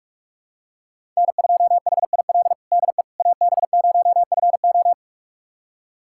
Morse code, heard as a keyed tone, about 46 wpm, 700 Hz. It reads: N2HIL DE AB0FO
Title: USA Callsign DE Callsign - 5 Characters - 45wpm